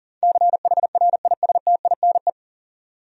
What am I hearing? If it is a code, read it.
Morse code, heard as a keyed tone, about 40 wpm, 700 Hz. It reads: CHRISTINE
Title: USA Common Names - 40wpm